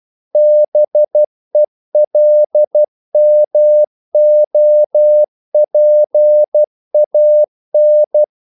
Morse code, heard as a keyed tone, about 12 wpm, 600 Hz. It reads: BELMOPAN
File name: CW / Practice / NationalCapitals12